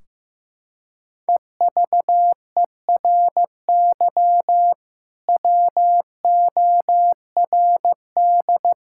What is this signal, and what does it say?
Morse code, heard as a keyed tone, about 14 wpm, 700 Hz. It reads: EVERY WORD
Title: Sentences from Top 1600 Words 15wpm